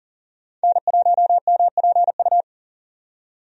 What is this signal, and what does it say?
Morse code, heard as a keyed tone, about 40 wpm, 700 Hz. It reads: N1MPU
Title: USA Call Signs - 5 Characters - 40wpm